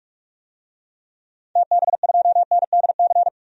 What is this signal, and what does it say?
Morse code, heard as a keyed tone, about 46 wpm, 700 Hz. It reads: TB2NDC
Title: DXCC-100 Call Signs - 6 Characters - 45wpm